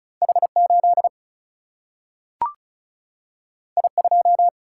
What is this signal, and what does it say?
Morse code, heard as a keyed tone, about 36 wpm, 700 Hz. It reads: H8 I2
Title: Single-Letter Single-Number 35wpm